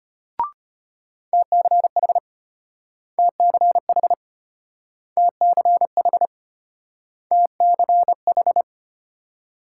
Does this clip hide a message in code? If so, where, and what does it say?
Yes: Morse code from 1.3 to 8.6 s, heard as a keyed tone, about 34 wpm, 700 Hz. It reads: TCH TCH TCH TCH